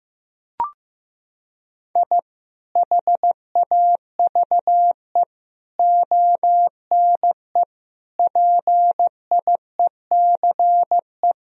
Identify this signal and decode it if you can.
Morse code, heard as a keyed tone, about 15 wpm, 700 Hz. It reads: I HAVE ONE PIECE
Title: Sentences from Top 1000 Words - Review Part 2 - 15wpm